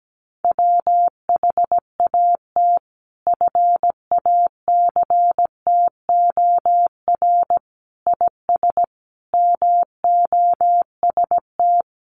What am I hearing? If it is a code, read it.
Morse code, heard as a keyed tone, about 17 wpm, 700 Hz. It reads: WHAT FACTOR IS MOST
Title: Sentences from Top 1300 Words 17wpm